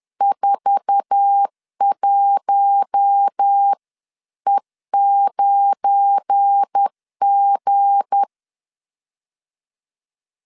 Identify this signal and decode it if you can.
Morse code, heard as a keyed tone, about 10 wpm, 800 Hz. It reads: 41E9G